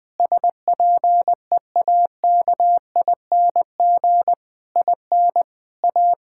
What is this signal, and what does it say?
Morse code, heard as a keyed tone, about 21 wpm, 700 Hz. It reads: SPEAKING IN A
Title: Difficult Sentences with Hints - 20wpm